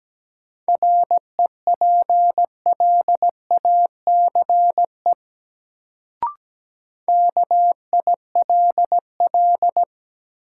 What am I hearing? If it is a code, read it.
Morse code, heard as a keyed tone, about 17 wpm, 700 Hz. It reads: REPLACE KILL
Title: Sentences from Top 1400 Words - New Word Drill - 17wpm